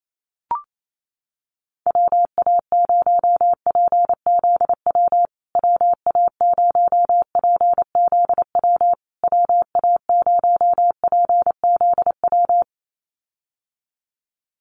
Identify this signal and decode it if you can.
Morse code, heard as a keyed tone, about 28 wpm, 700 Hz. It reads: WA0PZW WA0PZW WA0PZW
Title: USA Call Signs Repeated 3x 28wpm